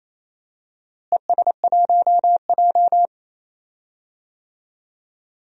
Morse code, heard as a keyed tone, about 28 wpm, 700 Hz. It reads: ES1J
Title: DXCC-100 Call Signs - 4 Characters - Repeated 3x - 28wpm